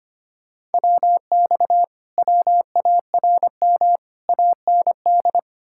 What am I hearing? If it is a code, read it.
Morse code, heard as a keyed tone, about 33 wpm, 700 Hz. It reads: WX WARM AND